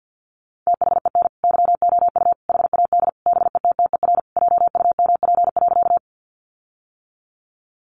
Morse code, heard as a keyed tone, about 50 wpm, 700 Hz. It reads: THEN YOU HAD BETTER JUMP.
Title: Sentences from Top 1300 Words 50wpm